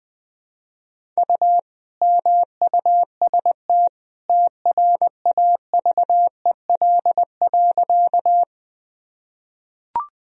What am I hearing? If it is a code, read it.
Morse code, heard as a keyed tone, about 21 wpm, 700 Hz. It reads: U MUST TRAVEL.
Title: Sentences from Top 400 Words and Abbreviations 20wpm